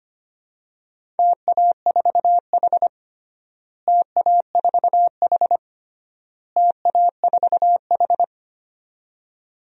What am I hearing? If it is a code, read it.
Morse code, heard as a keyed tone, about 25 wpm, 700 Hz. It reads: TA4H TA4H TA4H
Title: DXCC-100 Call Signs - 4 Characters - Repeated 3x - 25wpm